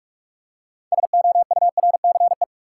Morse code, heard as a keyed tone, about 46 wpm, 700 Hz. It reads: SOURCE
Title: Six Letter Words - 45wpm